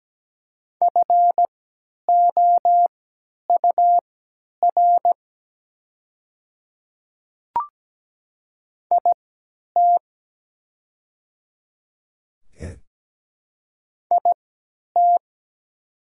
Morse code, heard as a keyed tone, about 17 wpm, spaced as for 10 wpm, 690 Hz. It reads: FOUR IT IT
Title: 3x Character Spacing - Top 500 Words - 17wpm